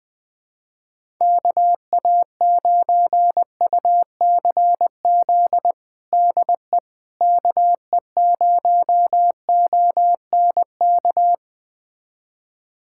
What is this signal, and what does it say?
Morse code, heard as a keyed tone, about 20 wpm, 700 Hz. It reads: KA9UCZ DE KE0ONK
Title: USA Callsign DE Callsign - 6 Characters - 20wpm